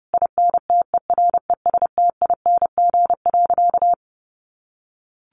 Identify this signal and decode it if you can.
Morse code, heard as a keyed tone, about 30 wpm, 700 Hz. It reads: INTERESTING.
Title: Sentences from Top 1200 Words 30wpm